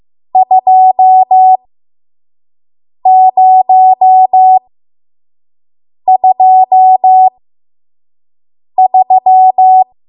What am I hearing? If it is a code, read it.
Morse code, heard as a keyed tone, about 15 wpm, spaced as for 5 wpm, 750 Hz. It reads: 2023
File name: Archive / 5 WPM / 05WPM